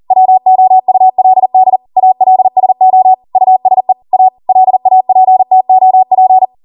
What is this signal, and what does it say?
Morse code, heard as a keyed tone, about 40 wpm, 750 Hz. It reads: WOULD ALSO USE A LAPTOP